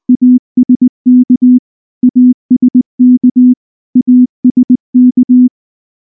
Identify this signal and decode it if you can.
Morse code, heard as a keyed tone, about 18 wpm, 250 Hz. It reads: ASK ASK ASK